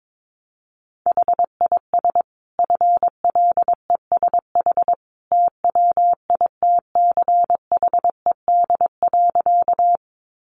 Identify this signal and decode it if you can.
Morse code, heard as a keyed tone, about 22 wpm, 700 Hz. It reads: HIS FLESH TWITCHED.
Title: A Sound Of Thunder 22wpm